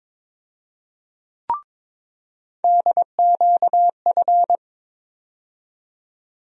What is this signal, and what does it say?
Morse code, heard as a keyed tone, about 22 wpm, 700 Hz. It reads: DQF
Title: USA Call Sign Postfixes - 3 Letters 22wpm